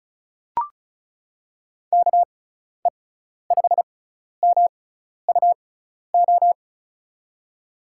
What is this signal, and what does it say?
Morse code, heard as a keyed tone, about 34 wpm, spaced as for 12 wpm, 700 Hz. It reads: KE5MUO
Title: USA Call Signs - 6x Character Spacing - 35wpm